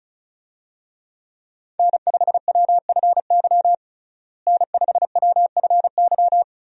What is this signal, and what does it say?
Morse code, heard as a keyed tone, about 36 wpm, 700 Hz. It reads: N5WFY N5WFY
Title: USA Call Signs - 5 Characters - Repeated 3x - 35wpm